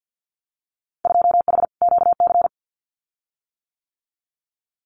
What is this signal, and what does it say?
Morse code, heard as a keyed tone, about 48 wpm, 700 Hz. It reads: 2H QC
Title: Winter Field Day Exchanges 50wpm